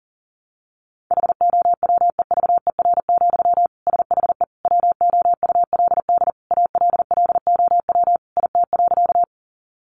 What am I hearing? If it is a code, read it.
Morse code, heard as a keyed tone, about 40 wpm, 700 Hz. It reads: HOWEVER, SHE WOULD ALLOW IT.
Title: Sentences from Top 700 Words 40wpm